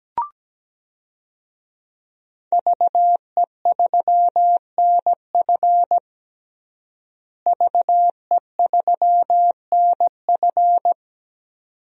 Morse code, heard as a keyed tone, about 17 wpm, 700 Hz. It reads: VE3NF VE3NF
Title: DXCC-100 Call Signs - 5 Characters - Repeated 2x - 17wpm